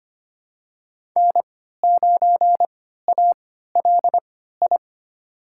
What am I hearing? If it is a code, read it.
Morse code, heard as a keyed tone, about 25 wpm, spaced as for 14 wpm, 700 Hz. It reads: N9ALI
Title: USA Call Signs - 3x Character Spacing - 25wpm